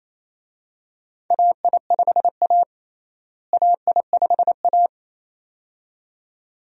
Morse code, heard as a keyed tone, about 28 wpm, 700 Hz. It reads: AI5A AI5A
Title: USA Call Signs - 4 Characters - Repeated 2x - 28wpm